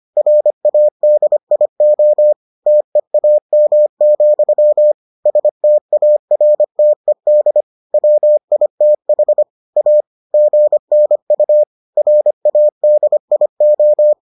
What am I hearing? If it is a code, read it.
Morse code, heard as a keyed tone, about 25 wpm, 600 Hz. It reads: RADIO TEAM, STARTED WITH A GNU RADIO